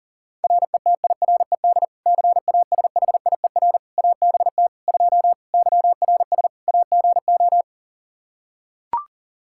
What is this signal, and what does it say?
Morse code, heard as a keyed tone, about 40 wpm, 700 Hz. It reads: RETIRED CASHIER ABT 2 YRS AGO
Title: K1USN SST QSO Practice 40wpm